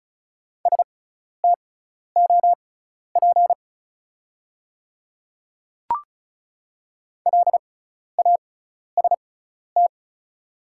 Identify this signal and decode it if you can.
Morse code, heard as a keyed tone, about 36 wpm, spaced as for 12 wpm, 690 Hz. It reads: STOP LAST
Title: 6x Character Spacing - Top 300 Words - 35wpm